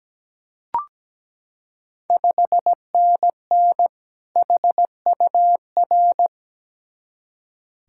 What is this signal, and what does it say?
Morse code, heard as a keyed tone, about 17 wpm, 700 Hz. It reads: 5NN HUR
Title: Ontario QSO Party Exchange 17wpm